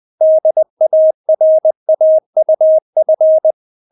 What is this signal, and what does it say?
Morse code, heard as a keyed tone, about 20 wpm, 630 Hz. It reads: DARAUF
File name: Witz1 20wpm